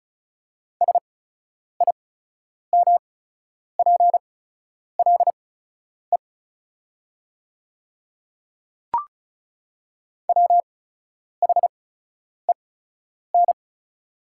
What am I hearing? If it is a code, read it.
Morse code, heard as a keyed tone, about 36 wpm, spaced as for 10 wpm, 700 Hz. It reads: SIMPLE WHEN
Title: 8x Character Spacing - Top 500 Words - 35wpm